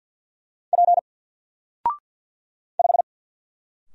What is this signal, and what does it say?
Morse code, heard as a keyed tone, about 48 wpm, 700 Hz. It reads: P 5
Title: Single Letter-Number Alt 50wpm